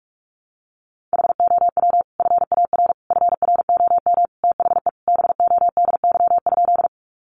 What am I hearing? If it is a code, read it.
Morse code, heard as a keyed tone, about 44 wpm, 700 Hz. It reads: HOW FAR FROM THE BODY?
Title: Sentences from Top 400 Words 45wpm